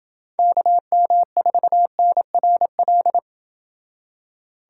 Morse code, heard as a keyed tone, about 26 wpm, 700 Hz. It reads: KM4NRL